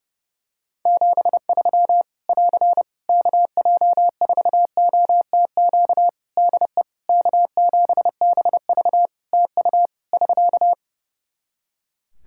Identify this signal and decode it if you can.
Morse code, heard as a keyed tone, about 30 wpm, 700 Hz. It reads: 73 <AR> KJ4OTQ DE K7BV TU <SK>
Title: K1USN SST QSO Practice 30wpm